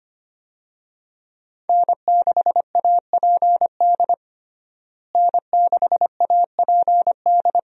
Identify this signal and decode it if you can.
Morse code, heard as a keyed tone, about 25 wpm, 700 Hz. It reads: N6APD N6APD
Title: USA Call Signs - 5 Characters - Repeated 2x - 25wpm